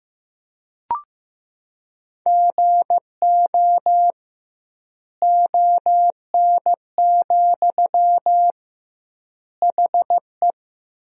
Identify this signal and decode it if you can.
Morse code, heard as a keyed tone, about 15 wpm, 700 Hz. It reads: GO ON, HE
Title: 2x Word-Spacing - Sentences from Top 100 Words - 15wpm